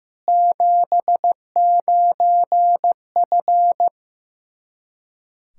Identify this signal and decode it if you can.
Morse code, heard as a keyed tone, about 15 wpm, 700 Hz. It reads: 79F
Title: Practice QSO - Exchange 2 - Info - 15wpm